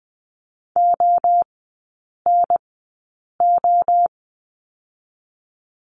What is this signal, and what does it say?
Morse code, heard as a keyed tone, about 20 wpm, 700 Hz. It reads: O N O
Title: 023 Lesson 5e - Letter o - Three Characters 20wpm